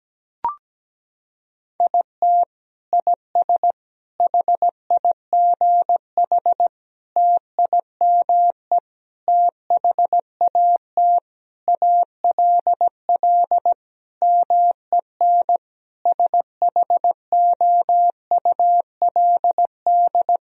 Morse code, heard as a keyed tone, about 17 wpm, 700 Hz. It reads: IT IS HIGH TIME THAT ALL MEN SHOULD
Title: Sentences from Top 1000 Words - Review Part 1 - 17wpm